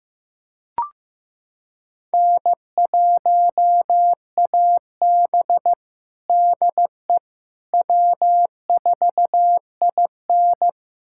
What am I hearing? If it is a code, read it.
Morse code, heard as a keyed tone, about 15 wpm, 700 Hz. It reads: N1AB DE W4IN
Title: USA Callsign DE Callsign - 4 Characters - 15wpm